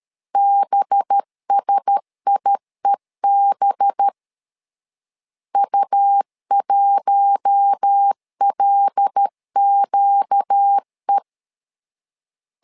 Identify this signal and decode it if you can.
Morse code, heard as a keyed tone, about 13 wpm, 800 Hz. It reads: BSIEB U1LQE
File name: LEZIONI / 5 LBVC / 5 LB 12wpm 1906